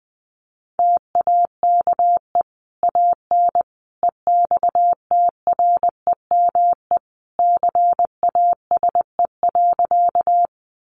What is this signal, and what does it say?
Morse code, heard as a keyed tone, about 20 wpm, 700 Hz. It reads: TAKE AN EXTREME CASE.
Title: Sentences from Top 1300 Words 20wpm